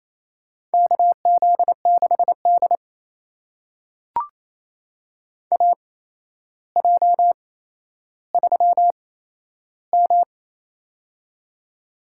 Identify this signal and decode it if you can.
Morse code, heard as a keyed tone, about 28 wpm, spaced as for 12 wpm, 700 Hz. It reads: KZ6D AJ3M
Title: USA Call Signs - 4 Characters - 8x Character Spacing - 28wpm